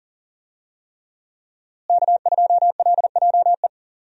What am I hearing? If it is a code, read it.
Morse code, heard as a keyed tone, about 40 wpm, 700 Hz. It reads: K2LJE
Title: USA Call Signs - 5 Characters - Repeated 3x - 40wpm